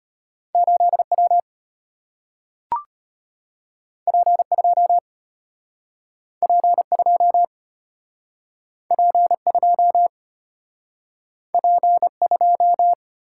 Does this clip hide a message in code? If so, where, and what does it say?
Yes: Morse code from 0.5 to 12.9 s, heard as a keyed tone, about 34 wpm, 700 Hz. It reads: 8W P2 P2 P2 P2